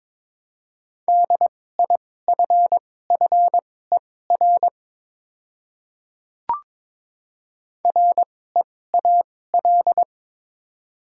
Morse code, heard as a keyed tone, about 22 wpm, spaced as for 16 wpm, 700 Hz. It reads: DIFFER REAL
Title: 2x Character Spacing - Top 300 Words - 22wpm